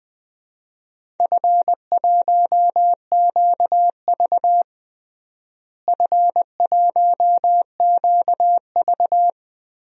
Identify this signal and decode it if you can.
Morse code, heard as a keyed tone, about 20 wpm, 700 Hz. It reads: F1QV F1QV
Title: DXCC-100 Call Signs - 4 Characters - Repeated 3x - 20wpm